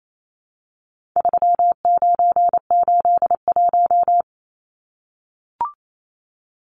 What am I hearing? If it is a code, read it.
Morse code, heard as a keyed tone, about 28 wpm, 700 Hz. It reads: 3981
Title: Numbers - 4-Digit 28wpm